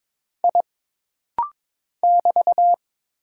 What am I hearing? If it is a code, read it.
Morse code, heard as a keyed tone, about 22 wpm, 700 Hz. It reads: I <BT>
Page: This rapid-fire timing is geared to push your ICR (Instant Character Recognition) skills.